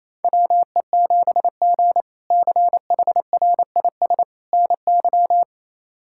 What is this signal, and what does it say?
Morse code, heard as a keyed tone, about 28 wpm, 700 Hz. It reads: WE7G CHRIS NY
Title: K1USN SST QSO Practice - 28wpm